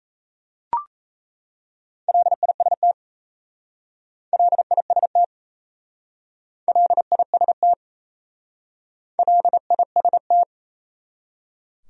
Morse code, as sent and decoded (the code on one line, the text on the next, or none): .-.. .. ... - / .-.. .. ... - / .-.. .. ... - / .-.. .. ... -
LIST LIST LIST LIST